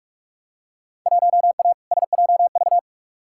Morse code, heard as a keyed tone, about 45 wpm, 700 Hz. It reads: 1A SJV
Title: Field Day Exchanges Encore - 45wpm